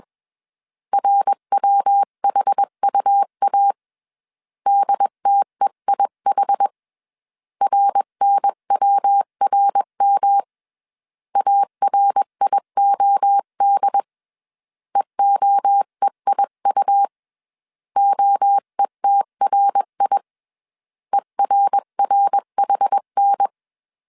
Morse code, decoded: RWHUA DTEIH RNWRM ARIOD EOEIU OETRI ERRHN

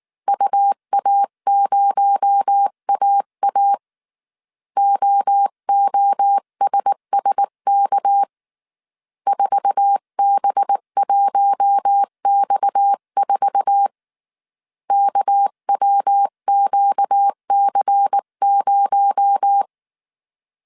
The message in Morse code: UA0AA OOSSK 4B1X4 KWQC0

